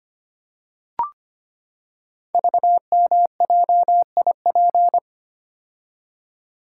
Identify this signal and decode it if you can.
Morse code, heard as a keyed tone, about 25 wpm, 700 Hz. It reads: VMJIP
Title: Code Groups - 5 Letter:Numbers - 25wpm